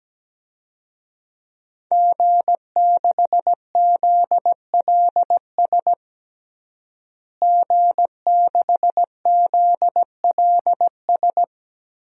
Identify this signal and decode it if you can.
Morse code, heard as a keyed tone, about 17 wpm, 700 Hz. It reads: G6ZLS G6ZLS
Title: DXCC-100 Call Signs - 5 Characters - Repeated 3x - 17wpm